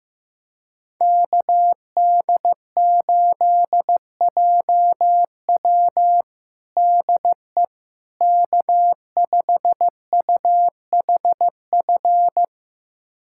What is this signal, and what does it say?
Morse code, heard as a keyed tone, about 15 wpm, 700 Hz. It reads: KD8JW DE K5UHF
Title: USA Callsign DE Callsign - 5 Characters - 15wpm